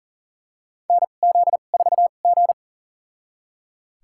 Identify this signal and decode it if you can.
Morse code, heard as a keyed tone, about 40 wpm, spaced as for 29 wpm, 700 Hz. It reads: NZ4G
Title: USA Call Signs - 4 Characters - 2x Character Spacing - 40wpm